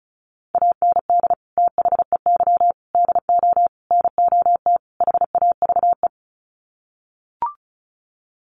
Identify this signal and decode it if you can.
Morse code, heard as a keyed tone, about 34 wpm, 700 Hz. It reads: AND THEY DO NOT HAVE
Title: Sets of 5 Words Encore2 35wpm